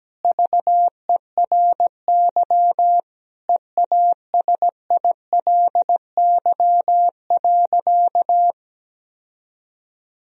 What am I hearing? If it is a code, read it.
Morse code, heard as a keyed tone, about 17 wpm, 700 Hz. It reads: VERY EASILY.
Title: Sentences from Top 1000 Words - Review Part 1 - 17wpm